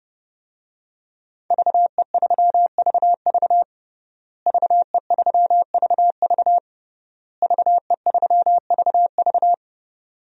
Morse code, heard as a keyed tone, about 32 wpm, 700 Hz. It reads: VE3VV VE3VV VE3VV
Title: DXCC-100 Call Signs - 5 Characters - Repeated 3x - 30wpm